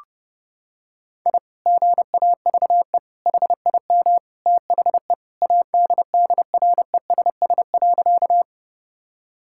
Morse code, heard as a keyed tone, about 30 wpm, 700 Hz. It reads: I GAVE HIM THE ADDRESS.
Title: Sentences from Top 1000 Words - Review Part 6 - 30wpm